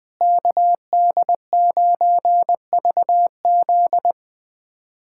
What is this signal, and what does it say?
Morse code, heard as a keyed tone, about 21 wpm, 700 Hz. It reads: KD9VZ
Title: USA Callsign DE Callsign - 5 Characters - 20wpm